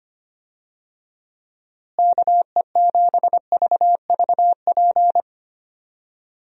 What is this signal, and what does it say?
Morse code, heard as a keyed tone, about 25 wpm, 700 Hz. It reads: KE7VVP
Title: USA Call Signs - 6 Characters - Repeated 3x - 25wpm